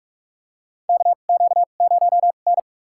Morse code, heard as a keyed tone, about 44 wpm, spaced as for 32 wpm, 690 Hz. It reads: KQ0N
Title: USA Call Signs - 4 Characters - 2x Character Spacing - 45wpm